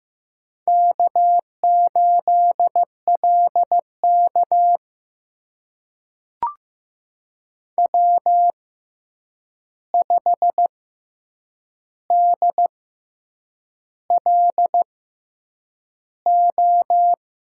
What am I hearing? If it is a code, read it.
Morse code, heard as a keyed tone, about 15 wpm, spaced as for 7 wpm, 700 Hz. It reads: K8LK W5DLO
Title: USA Call Signs - 6x Character Spacing - 15wpm